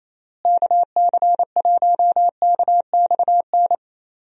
Morse code, heard as a keyed tone, about 27 wpm, 700 Hz. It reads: KC1KXN
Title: USA Call Signs - 6 Characters - Repeated 2x - 28wpm